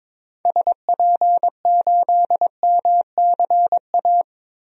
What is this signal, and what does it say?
Morse code, heard as a keyed tone, about 22 wpm, 700 Hz. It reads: SP8MCA
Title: DXCC-100 Call Signs - 6 Characters - Repeated 3x - 22wpm